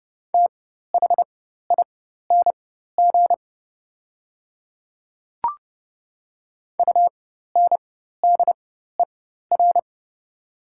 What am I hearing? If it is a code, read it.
Morse code, heard as a keyed tone, about 30 wpm, spaced as for 14 wpm, 700 Hz. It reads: THING UNDER
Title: 4x Character Spacing - Top 200 Words - 30wpm